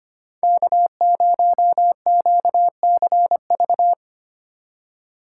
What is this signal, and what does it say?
Morse code, heard as a keyed tone, about 25 wpm, 700 Hz. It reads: K0QCV
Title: USA Call Signs - 2x Character Spacing - 25wpm